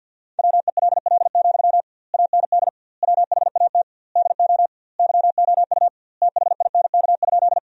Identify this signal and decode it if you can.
Morse code, heard as a keyed tone, about 52 wpm, 700 Hz. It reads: WELL, AND WHAT DO YOU THINK?
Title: 2x Word-Spacing - Sentences from Top 200 Words - 50wpm